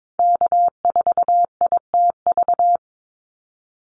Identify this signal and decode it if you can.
Morse code, heard as a keyed tone, about 22 wpm, 700 Hz. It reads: K4ITV
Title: Call Signs Repeated 3x - 22wpm